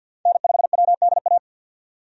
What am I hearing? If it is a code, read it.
Morse code, heard as a keyed tone, about 52 wpm, 700 Hz. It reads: N5WDA
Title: USA Call Signs - 5 Characters - 50wpm